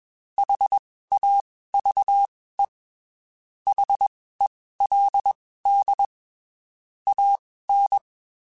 Morse code, heard as a keyed tone, about 21 wpm, spaced as for 15 wpm, 800 Hz. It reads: HAVE HELD AN
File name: (spaced) / QSOs Spaced-mp3s / kc7sac @20 sp